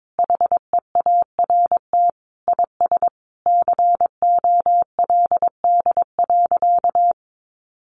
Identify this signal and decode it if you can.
Morse code, heard as a keyed tone, about 22 wpm, 700 Hz. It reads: HEART IS COLD.
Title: Sentences from Top 1200 Words 22wpm